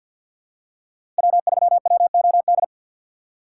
Morse code, heard as a keyed tone, about 51 wpm, 700 Hz. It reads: W3WOD
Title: USA Call Signs - 5 Characters - 3x Character Spacing - 50wpm